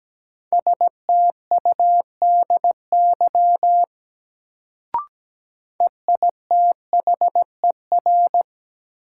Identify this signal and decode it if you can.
Morse code, heard as a keyed tone, about 17 wpm, 700 Hz. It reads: STUDY EITHER
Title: Sets of 5 Words - New Word Drill - 17wpm